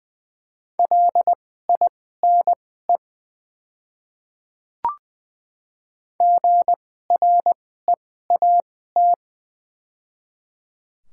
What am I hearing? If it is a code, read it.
Morse code, heard as a keyed tone, about 20 wpm, spaced as for 14 wpm, 700 Hz. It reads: LINE GREAT
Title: 2x Character Spacing - Top 300 Words - 20wpm